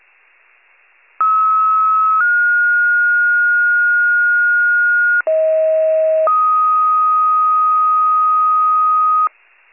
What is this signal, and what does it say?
Two-tone page, heard as a one-second tone, then a three-second tone, about 1290 then 1430 Hz, about 640 then 1250 Hz. It is heard from 1.2 to 9.3 s.
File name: pager/84 FIRE